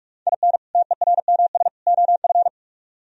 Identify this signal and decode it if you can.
Morse code, heard as a keyed tone, about 44 wpm, 700 Hz. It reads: IN TERMS OF